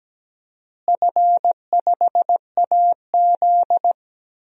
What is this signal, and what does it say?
Morse code, heard as a keyed tone, about 17 wpm, 700 Hz. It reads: F5AZ
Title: DXCC-100 Call Signs - 4 Characters - Repeated 2x - 17wpm